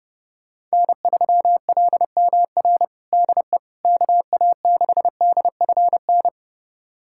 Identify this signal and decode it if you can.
Morse code, heard as a keyed tone, about 29 wpm, 700 Hz. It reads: N3LMR DE KA6DFN
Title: USA Callsign DE Callsign 30wpm